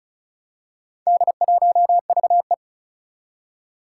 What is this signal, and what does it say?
Morse code, heard as a keyed tone, about 36 wpm, 700 Hz. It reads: D1VE
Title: Four Challenging Letter-Number Combinations 35wpm